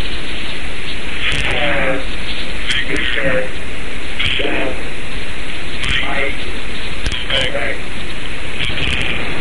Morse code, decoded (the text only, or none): none